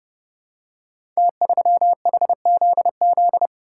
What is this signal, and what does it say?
Morse code, heard as a keyed tone, about 30 wpm, 700 Hz. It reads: T3HZZ
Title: Code Groups - 5 Letter:Numbers - 30wpm